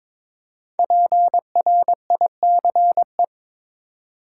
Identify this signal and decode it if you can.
Morse code, heard as a keyed tone, about 22 wpm, 700 Hz. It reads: PRICE
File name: 5 Letter Words - First 3 Letters Given - 22wpm